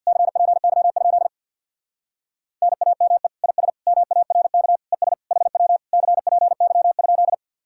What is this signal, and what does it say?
Morse code, heard as a keyed tone, about 62 wpm, 700 Hz. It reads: CPY? NAME IS MARK ES HW CPY?
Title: Speed-Racing Common QSO Phrases Encore - 50wpm